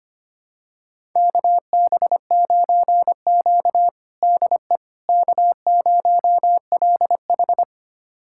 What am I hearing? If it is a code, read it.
Morse code, heard as a keyed tone, about 25 wpm, 700 Hz. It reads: KB9Q DE K0LH
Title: USA Callsign DE Callsign - 4 Characters - 25wpm